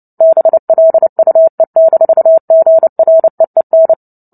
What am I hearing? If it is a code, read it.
Morse code, heard as a keyed tone, about 30 wpm, 650 Hz. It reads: BLUE-GREEN
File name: Book Two / 03-Chapter 3 - 29 WPM